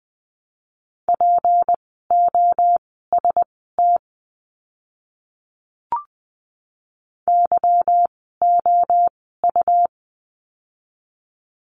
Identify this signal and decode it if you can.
Morse code, heard as a keyed tone, about 20 wpm, spaced as for 14 wpm, 700 Hz. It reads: POST YOU
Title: 153 Lesson 25f - Letter p - Words 20wpm